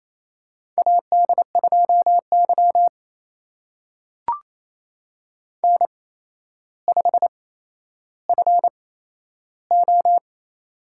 Morse code, heard as a keyed tone, about 28 wpm, spaced as for 12 wpm, 700 Hz. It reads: AD2Y N5FO
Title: USA Call Signs - 4 Characters - 8x Character Spacing - 28wpm